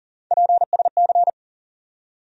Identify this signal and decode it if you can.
Morse code, heard as a keyed tone, about 40 wpm, 700 Hz. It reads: PSC
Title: USA Call Sign Postfixes - 3 Letters 40wpm